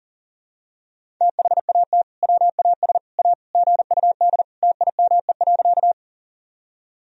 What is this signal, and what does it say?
Morse code, heard as a keyed tone, about 40 wpm, 700 Hz. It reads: THAT WAS A GUD TIME.
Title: Sentences from Top 200 Words and Abbreviations 40wpm